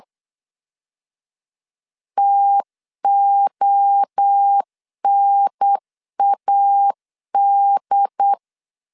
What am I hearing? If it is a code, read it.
Morse code, heard as a keyed tone, about 8 wpm, 800 Hz. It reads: TONAD